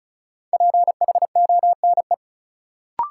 Morse code, heard as a keyed tone, about 34 wpm, 700 Hz. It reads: PHONE
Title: Sentences from Top 1100 Words - New Word Drill - 35wpm